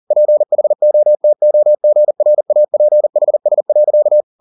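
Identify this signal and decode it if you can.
Morse code, heard as a keyed tone, about 40 wpm, 600 Hz. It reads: PHOTOGRAPHS.
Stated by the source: Morse code at 40 wpm, practice text, CW med Gnister.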